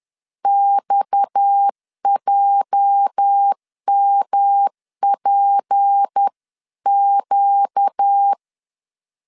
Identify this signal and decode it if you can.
Morse code, heard as a keyed tone, about 10 wpm, 800 Hz. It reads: XJMPQ
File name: LEZIONI / 8 YJ45 / 8 YJ4 10wpm 2633